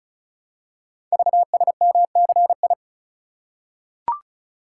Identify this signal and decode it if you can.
Morse code, heard as a keyed tone, about 36 wpm, 700 Hz. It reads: VSMCI
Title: Code Groups - 5 Letter:Numbers - 35wpm